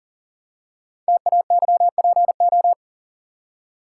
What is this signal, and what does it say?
Morse code, heard as a keyed tone, about 41 wpm, 690 Hz. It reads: TAYPO
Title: Code Groups - 5 Letters - 40wpm